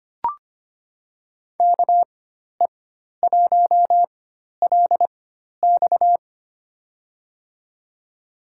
Morse code, heard as a keyed tone, about 25 wpm, spaced as for 12 wpm, 700 Hz. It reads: KE1LX
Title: USA Call Signs - 5 Characters - 4x Character Spacing - 25wpm